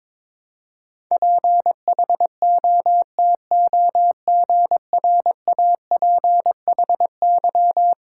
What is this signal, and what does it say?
Morse code, heard as a keyed tone, about 22 wpm, 700 Hz. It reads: PHOTOGRAPHY
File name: practice / 22wpm / Eleven Letter Words - 22wpm